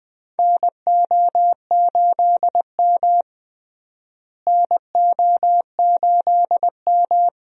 Morse code, heard as a keyed tone, about 21 wpm, 700 Hz. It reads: NO8M NO8M
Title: USA Call Signs - 4 Characters - Repeated 2x - 20wpm